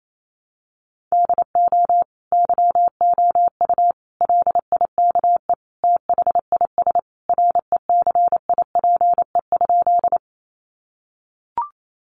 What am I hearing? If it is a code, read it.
Morse code, heard as a keyed tone, about 28 wpm, 700 Hz. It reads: DO YOU LIKE THIS RECIPE?
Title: Sentences from Top 1200 Words 28wpm